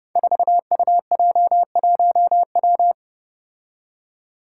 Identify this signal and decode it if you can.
Morse code, heard as a keyed tone, about 30 wpm, 700 Hz. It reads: 4UJ1W
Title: Five Challenging Letter-Number Combinations 30wpm